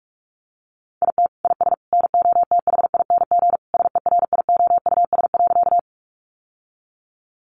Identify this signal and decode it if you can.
Morse code, heard as a keyed tone, about 46 wpm, 700 Hz. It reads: IT IS NOTHING SERIOUS.
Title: Sentences from Top 1100 Words 45wpm